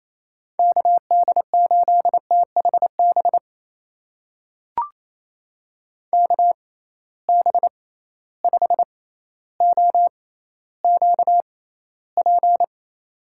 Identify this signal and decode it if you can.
Morse code, heard as a keyed tone, about 28 wpm, spaced as for 14 wpm, 700 Hz. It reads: KD8THB KB5OQP
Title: USA Call Signs - 6 Characters - 6x Character Spacing - 28wpm